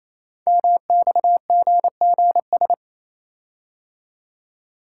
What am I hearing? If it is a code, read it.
Morse code, heard as a keyed tone, about 27 wpm, 700 Hz. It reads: MXGGS
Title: Code Groups - 5 Letters - 28wpm